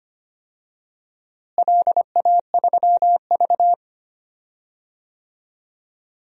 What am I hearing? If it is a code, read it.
Morse code, heard as a keyed tone, about 25 wpm, 700 Hz. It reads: LA3V
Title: DXCC-100 Call Signs - 4 Characters - Repeated 3x - 25wpm